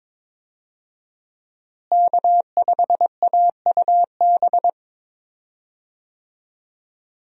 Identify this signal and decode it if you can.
Morse code, heard as a keyed tone, about 22 wpm, 700 Hz. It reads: K5AUB
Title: USA Call Signs - 5 Characters - Repeated 2x - 22wpm